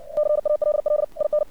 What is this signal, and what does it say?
Morse code, heard as a keyed tone, about 59 wpm, 600 Hz. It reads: CALL IN
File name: archived / CRSnet